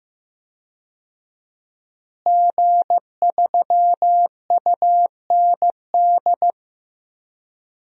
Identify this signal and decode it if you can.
Morse code, heard as a keyed tone, about 15 wpm, 700 Hz. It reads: G3UND
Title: DXCC-100 Call Signs - 5 Characters - Repeated 3x - 15wpm